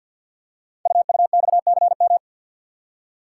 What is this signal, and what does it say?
Morse code, heard as a keyed tone, about 50 wpm, 690 Hz. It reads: UUXCM
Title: Code Groups - 5 Letters - 50wpm